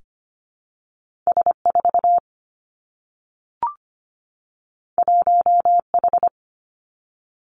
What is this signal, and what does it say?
Morse code, heard as a keyed tone, about 25 wpm, 700 Hz. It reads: S4 1H
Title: Two Challenging Letter-Number Combinations 25wpm